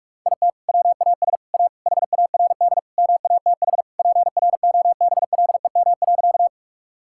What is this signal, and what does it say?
Morse code, heard as a keyed tone, about 46 wpm, 700 Hz. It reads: IT WAS A HARD MATH PROBLEM.
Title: Sentences from Top 1000 Words - Review Part 6 - 45wpm